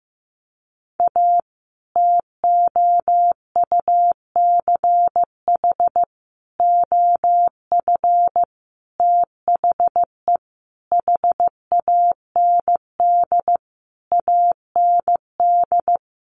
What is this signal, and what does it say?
Morse code, heard as a keyed tone, about 15 wpm, 700 Hz. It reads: A TOUCH OF THE HAND AND
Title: A Sound Of Thunder 15wpm